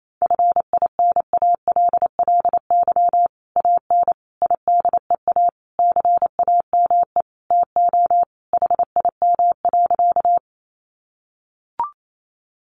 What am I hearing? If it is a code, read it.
Morse code, heard as a keyed tone, about 28 wpm, 700 Hz. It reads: FINALLY AN IDEA CAME TO HIM.
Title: Sentences from Top 1000 Words 28wpm